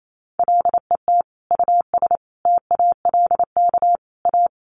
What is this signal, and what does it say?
Morse code, heard as a keyed tone, about 28 wpm, 700 Hz. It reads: LET US TALK A
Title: Sentences from Top 400 Words 28wpm